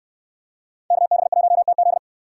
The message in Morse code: KB1TZ

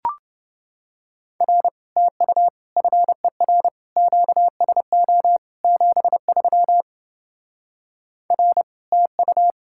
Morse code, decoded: R TU FER QSO 73 R TU

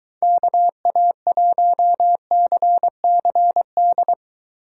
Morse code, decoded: KA1CCD

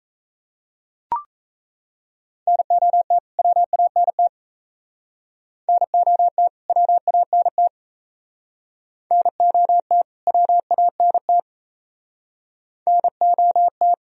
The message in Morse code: NOT WANT NOT WANT NOT WANT NOT

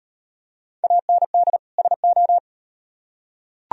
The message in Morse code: AND SO

